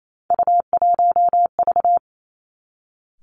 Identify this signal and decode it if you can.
Morse code, heard as a keyed tone, about 28 wpm, 700 Hz. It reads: U1V
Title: Three Challenging Letter-Number Combinations 28wpm